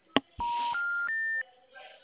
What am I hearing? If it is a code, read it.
Special information tone, with noise.